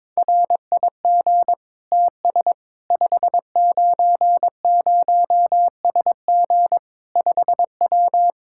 Morse code, heard as a keyed tone, about 22 wpm, 700 Hz. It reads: RIG TS 590SG 5W
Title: Practice QSO - Exchange 2 - Info - 22wpm